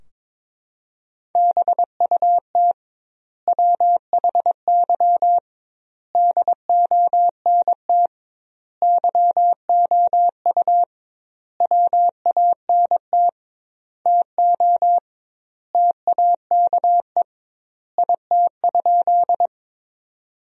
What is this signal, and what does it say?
Morse code, heard as a keyed tone, about 22 wpm, 700 Hz. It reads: BUT WHY DONT YOU WANT TO TAKE IT?
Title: 2x Word-Spacing - Sentences from Top 300 Words - 22wpm